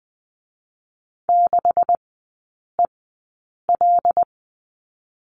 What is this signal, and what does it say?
Morse code, heard as a keyed tone, about 20 wpm, spaced as for 8 wpm, 700 Hz. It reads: 6EL
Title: 127 Lesson 21e - Number 6 - Three Characters 20wpm